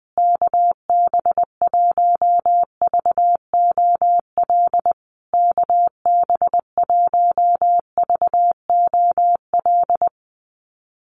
Morse code, heard as a keyed tone, about 20 wpm, 700 Hz. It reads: KB1VOL KB1VOL
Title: USA Call Signs Repeated 3x 20wpm